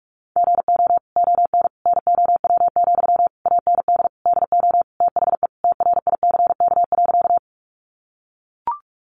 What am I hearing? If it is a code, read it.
Morse code, heard as a keyed tone, about 44 wpm, 700 Hz. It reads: GO ON NOW, AND DO THE TRICK.
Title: Sentences from Top 1500 Words 45wpm